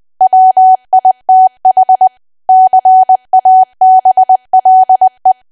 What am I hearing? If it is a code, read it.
Morse code, heard as a keyed tone, about 20 wpm, 750 Hz. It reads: WITH CABLE
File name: cwa / W1AW-Level3 / 20WPM